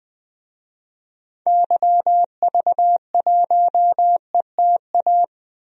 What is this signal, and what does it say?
Morse code, heard as a keyed tone, about 20 wpm, 700 Hz. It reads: YV1ETA
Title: DXCC-100 Call Signs - 6 Characters - 20wpm